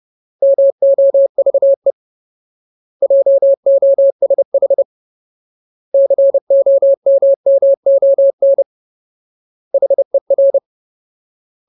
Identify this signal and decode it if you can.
Morse code, heard as a keyed tone, about 30 wpm, 550 Hz. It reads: MOVE JOSH COMMON HER
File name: Ultimate Word List / 30WPM Ultimate